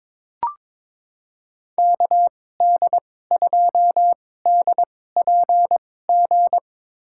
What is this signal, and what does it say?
Morse code, heard as a keyed tone, about 22 wpm, spaced as for 16 wpm, 700 Hz. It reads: KD2DPG
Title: USA Call Signs - 6 Characters - 2x Character Spacing - 22wpm